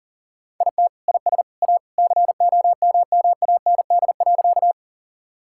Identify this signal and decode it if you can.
Morse code, heard as a keyed tone, about 40 wpm, 700 Hz. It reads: IT IS A COMMAND.
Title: Sentences from Top 1600 Words 40wpm